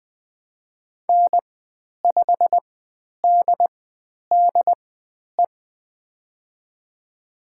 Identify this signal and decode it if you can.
Morse code, heard as a keyed tone, about 20 wpm, spaced as for 10 wpm, 700 Hz. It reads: N5DDE
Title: 109 Lesson 18g - Letter m - Call Signs 20wpm